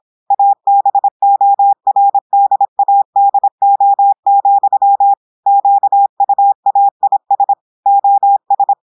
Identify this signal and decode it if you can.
Morse code, heard as a keyed tone, about 26 wpm, 800 Hz. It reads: ABORDADO, QUAIS OS
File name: portugais texte en clair 2 vitesse 26 mots minutes